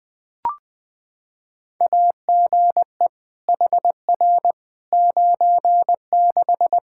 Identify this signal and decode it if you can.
Morse code, heard as a keyed tone, about 20 wpm, 700 Hz. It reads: AGE HR 96
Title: Practice QSO - Exchange 3 - More Info - 20wpm